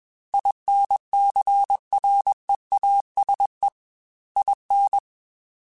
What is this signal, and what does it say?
Morse code, heard as a keyed tone, about 21 wpm, 800 Hz. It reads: INCREASE IN
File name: Technical Subjects / Technical Subjects-mp3s / XC @20